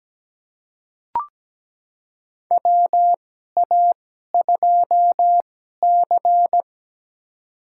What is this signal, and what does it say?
Morse code, heard as a keyed tone, about 17 wpm, spaced as for 12 wpm, 700 Hz. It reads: WA2C
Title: USA Call Signs - 4 Characters - 2x Character Spacing - 17wpm